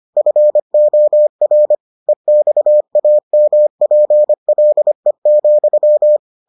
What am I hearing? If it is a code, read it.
Morse code, heard as a keyed tone, about 25 wpm, 600 Hz. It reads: FOR EXAMPLE,